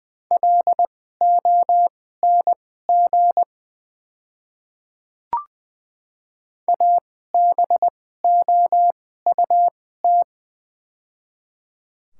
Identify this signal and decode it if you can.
Morse code, heard as a keyed tone, about 20 wpm, spaced as for 15 wpm, 700 Hz. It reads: LONG ABOUT
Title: 2x Character Spacing - Top 100 Words - 20wpm